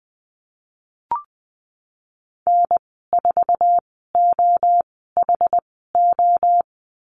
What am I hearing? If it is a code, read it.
Morse code, heard as a keyed tone, about 20 wpm, spaced as for 15 wpm, 700 Hz. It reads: N4OHO
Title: 069 Lesson 12g - Letter d - Call Signs 20wpm